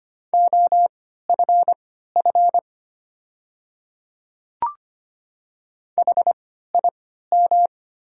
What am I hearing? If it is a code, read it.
Morse code, heard as a keyed tone, about 25 wpm, spaced as for 14 wpm, 700 Hz. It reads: OFF HIM
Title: 3x Character Spacing - Top 200 Words - 25wpm